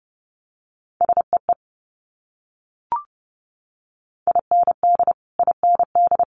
Morse code, heard as a keyed tone, about 30 wpm, 690 Hz. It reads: SEE IND IND
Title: Multi-letter Phonemes Repeated 3x 30wpm